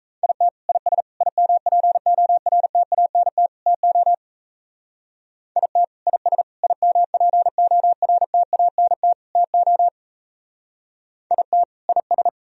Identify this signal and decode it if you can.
Morse code, heard as a keyed tone, about 32 wpm, 700 Hz. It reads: IT IS IMPORTANT TO IT IS IMPORTANT TO IT IS
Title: Speed-Racing - Sets of 4 Words 28wpm